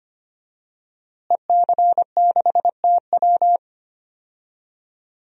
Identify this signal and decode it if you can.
Morse code, heard as a keyed tone, about 25 wpm, 700 Hz. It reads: EC6TW
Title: DXCC-100 Call Signs - 5 Characters - Repeated 3x - 25wpm